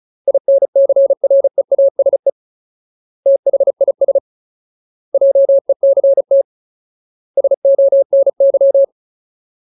Morse code, heard as a keyed tone, about 34 wpm, 550 Hz. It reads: INCREASE THIS JECT SONY
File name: Ultimate Word List / 35WPM Ultimate